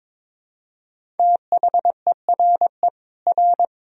Morse code, heard as a keyed tone, about 22 wpm, 700 Hz. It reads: THERE R
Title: Sentences from Top 300 Words and Abbreviations 22wpm